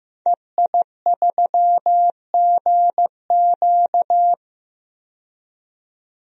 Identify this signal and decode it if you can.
Morse code, heard as a keyed tone, about 15 wpm, 700 Hz. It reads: EI3GQ
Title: DXCC-100 Call Signs - 5 Characters - 15wpm